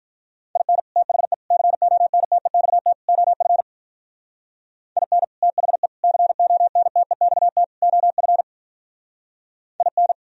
Morse code, heard as a keyed tone, about 45 wpm, 700 Hz. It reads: IN THE CONTEXT OF IN THE CONTEXT OF IN